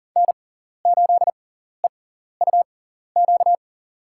Morse code, heard as a keyed tone, about 40 wpm, spaced as for 14 wpm, 700 Hz. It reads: N8EUQ
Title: USA Call Signs - 5 Characters - 6x Character Spacing - 40wpm